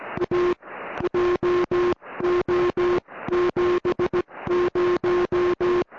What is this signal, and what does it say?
Morse code, heard as a keyed tone, about 17 wpm, spaced as for 15 wpm, 360 Hz, with noise. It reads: A JO 70
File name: au / oz9pz ok1rk 70cm au